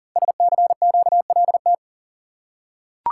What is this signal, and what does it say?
Morse code, heard as a keyed tone, about 39 wpm, 690 Hz. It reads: SCQLT
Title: Code Groups - 5 Letters - 40wpm